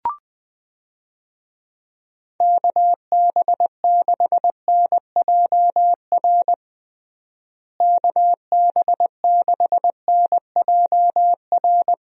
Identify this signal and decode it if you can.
Morse code, heard as a keyed tone, about 19 wpm, 700 Hz. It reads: KB6NJR KB6NJR
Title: USA Call Signs - 6 Characters - Repeated 3x - 20wpm